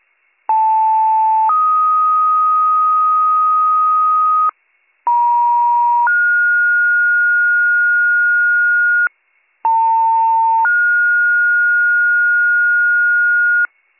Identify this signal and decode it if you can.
Two-tone page, heard as a one-second tone, then a three-second tone, about 880 then 1250 Hz, about 950 then 1430 Hz, about 900 then 1430 Hz, with noise. It is heard from 0.5 to 13.7 s.